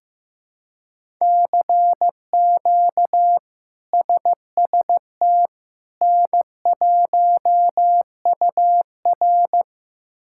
Morse code, heard as a keyed tone, about 15 wpm, 700 Hz. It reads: CQ SST N1UR
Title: K1USN SST QSO Practice - 15wpm